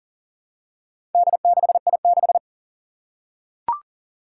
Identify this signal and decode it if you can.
Morse code, heard as a keyed tone, about 37 wpm, 700 Hz. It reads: D6I6
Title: Four Challenging Letter-Number Combinations 40wpm